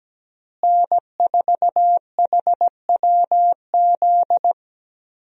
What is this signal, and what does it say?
Morse code, heard as a keyed tone, about 17 wpm, 700 Hz. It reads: N4HWZ
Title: USA Call Signs - 5 Characters - Repeated 2x - 17wpm